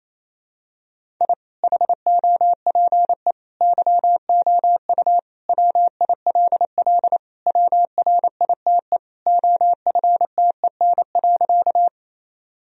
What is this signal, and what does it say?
Morse code, heard as a keyed tone, about 28 wpm, 700 Hz. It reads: I HOPE YOU WILL WRITE OFTEN.
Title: Sentences from Top 1000 Words - Review Part 1 - 28wpm